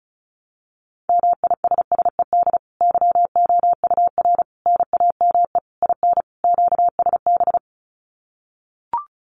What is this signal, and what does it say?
Morse code, heard as a keyed tone, about 34 wpm, 700 Hz. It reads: MISSED YOUR NAME IN QSB
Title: Common QSO Phrases 35wpm